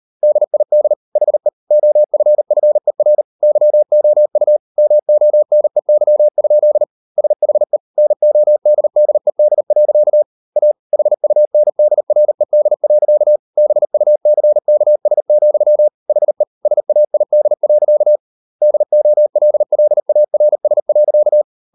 Morse code, heard as a keyed tone, about 38 wpm, 600 Hz. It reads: DID HE OFFER YOU MONEY? SHE NODDED. A HUNDRED. BUCKS, HE SAID. DOLLARS.